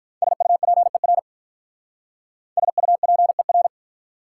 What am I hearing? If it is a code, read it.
Morse code, heard as a keyed tone, about 43 wpm, 690 Hz. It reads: SUPER SUPER
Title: Speed-Racing Prefixes and Suffixes 35wpm